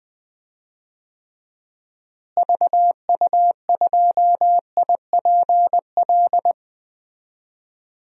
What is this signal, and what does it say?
Morse code, heard as a keyed tone, about 21 wpm, 700 Hz. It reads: VU2IPL
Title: DXCC-100 Call Signs - 6 Characters - Repeated 3x - 20wpm